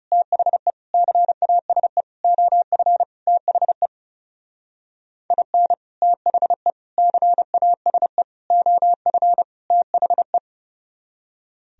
Morse code, heard as a keyed tone, about 38 wpm, 700 Hz. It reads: THE CASE OF THE IN THE CASE OF THE